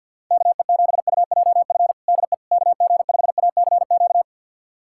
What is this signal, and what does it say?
Morse code, heard as a keyed tone, about 53 wpm, 700 Hz. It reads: KE7UJF DE KG5ACQ
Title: USA Callsign DE Callsign - 6 Characters - 50wpm